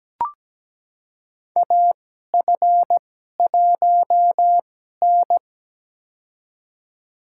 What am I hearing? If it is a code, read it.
Morse code, heard as a keyed tone, about 17 wpm, spaced as for 12 wpm, 700 Hz. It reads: AF1N
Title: USA Call Signs - 4 Characters - 2x Character Spacing - 17wpm